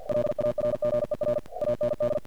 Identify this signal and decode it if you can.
Morse code, heard as a keyed tone, about 50 wpm, 600 Hz. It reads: LARGER AND
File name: archived / CRSnet